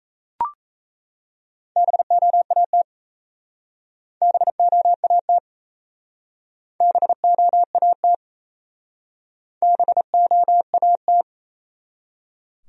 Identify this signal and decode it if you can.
Morse code, heard as a keyed tone, about 36 wpm, 700 Hz. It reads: BOAT BOAT BOAT BOAT